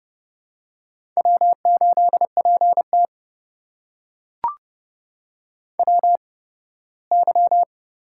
Morse code, heard as a keyed tone, about 30 wpm, spaced as for 18 wpm, 700 Hz. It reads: W8PT WY
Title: USA Call Signs - 4 Characters - 8x Character Spacing - 30wpm